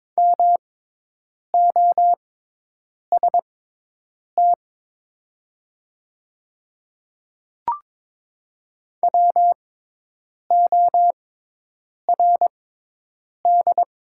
Morse code, heard as a keyed tone, about 21 wpm, spaced as for 8 wpm, 700 Hz. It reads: MOST WORD
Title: 6x Character Spacing - Top 100 Words - 22wpm